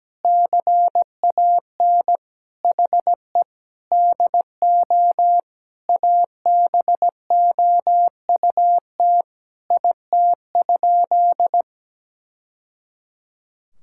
Morse code, heard as a keyed tone, about 17 wpm, 700 Hz. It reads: CAN HE DO ABOUT IT?
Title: Sentences from Top 1000 Words - Review Part 6 - 17wpm